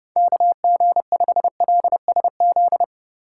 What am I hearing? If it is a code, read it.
Morse code, heard as a keyed tone, about 38 wpm, 700 Hz. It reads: KG5LSZ